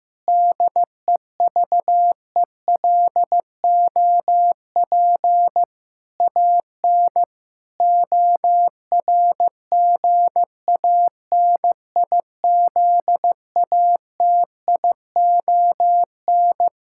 Morse code, heard as a keyed tone, about 15 wpm, 700 Hz. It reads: DEVELOP AN ORGANIZATION
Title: Sentences from Top 1000 Words - Review Part 5 - 15wpm